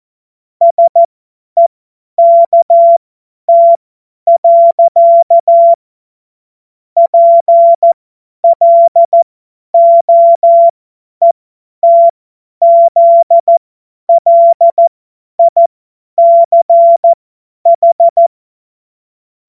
Morse code, heard as a keyed tone, about 14 wpm, spaced as for 10 wpm, 680 Hz. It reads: SEKT. PLOETZLICH
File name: Morsekurs / Witze / Witz 19 Dezember 2023 50 F70